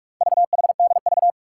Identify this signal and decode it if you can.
Morse code, heard as a keyed tone, about 44 wpm, 700 Hz. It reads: VHDV
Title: Four Challenging Letter-Number Combinations 45wpm